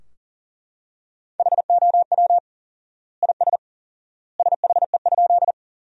Morse code, heard as a keyed tone, about 37 wpm, 700 Hz. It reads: HOW IS SHE?
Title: 4x Word-Spacing - Sentences from Top 100 Words - 40wpm